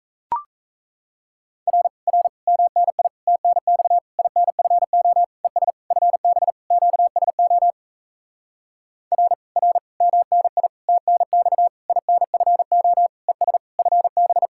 Morse code, heard as a keyed tone, about 37 wpm, 700 Hz. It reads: R R MNI TNX INFO ES FB QSO R R MNI TNX INFO ES FB